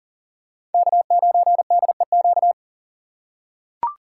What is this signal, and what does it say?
Morse code, heard as a keyed tone, about 40 wpm, 700 Hz. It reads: K9DEQ
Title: USA Call Signs - 5 Characters - 3x Character Spacing - 40wpm